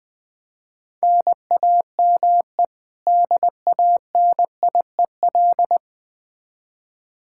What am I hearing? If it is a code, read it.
Morse code, heard as a keyed tone, about 20 wpm, 700 Hz. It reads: NAME DANIEL
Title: Practice QSO - Exchange 1 - Beginning - 20wpm